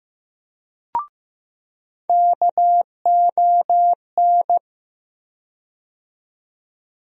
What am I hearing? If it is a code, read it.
Morse code, heard as a keyed tone, about 15 wpm, 700 Hz. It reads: KON
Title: USA Call Sign Postfixes - 3 Letters 15wpm